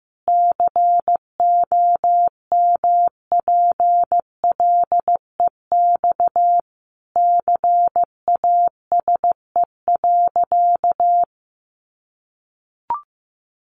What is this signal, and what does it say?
Morse code, heard as a keyed tone, about 14 wpm, 700 Hz. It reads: COMPLEX CASE.
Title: Sentences from Top 900 Words 15wpm